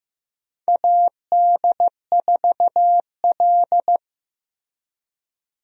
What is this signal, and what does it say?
Morse code, heard as a keyed tone, about 15 wpm, 700 Hz. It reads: AD4L
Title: USA Call Signs - 4 Characters - Repeated 3x - 15wpm